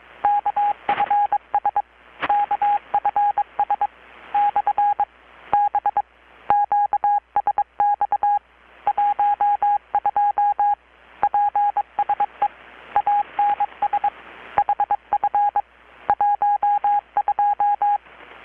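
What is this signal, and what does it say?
Morse code, heard as a keyed tone, about 22 wpm, 820 Hz. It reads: KFS KFS / B QSX 12 PSE ANS HF 12
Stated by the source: Morse from KPH's Night of Nights XX maritime radiotelegraphy event, received in Chiapas, Mexico about 3,800 km away on a Kenwood R-600 with a grounded 10 m long-wire antenna.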